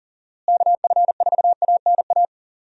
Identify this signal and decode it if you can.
Morse code, heard as a keyed tone, about 40 wpm, 700 Hz. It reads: KF4ANA
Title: USA Call Signs - 6 Characters - Repeated 3x - 40wpm